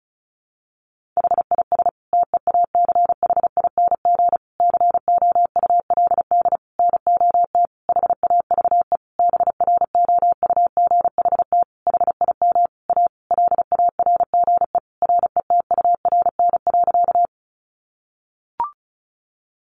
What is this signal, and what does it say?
Morse code, heard as a keyed tone, about 34 wpm, 700 Hz. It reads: HIS TEACHING COULD NOT HAVE BROUGHT HIM A LARGE RETURN.
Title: Sentences from Top 1100 Words 35wpm